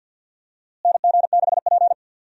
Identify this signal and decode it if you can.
Morse code, heard as a keyed tone, about 52 wpm, 700 Hz. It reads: NG6P
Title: USA Call Signs - 4 Characters - Repeated 3x - 50wpm